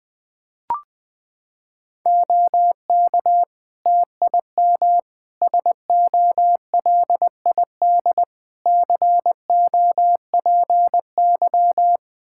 Morse code, heard as a keyed tone, about 21 wpm, 700 Hz. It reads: OK TIM SOLID COPY
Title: Practice QSO - Exchange 3 - More Info - 20wpm